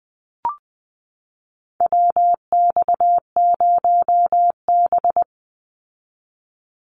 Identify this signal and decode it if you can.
Morse code, heard as a keyed tone, about 20 wpm, 700 Hz. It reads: WX0B
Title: Call Signs Repeated 2x - 20wpm